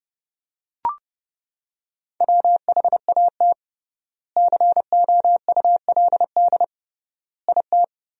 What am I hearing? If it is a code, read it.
Morse code, heard as a keyed tone, about 30 wpm, 700 Hz. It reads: WHAT COULD IT
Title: 3x Word-Spacing - Sentences from Top 100 Words - 30wpm